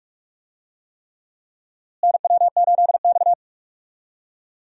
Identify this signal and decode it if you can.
Morse code, heard as a keyed tone, about 46 wpm, 700 Hz. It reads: NW8X
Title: USA Call Signs - 4 Characters - Repeated 3x - 45wpm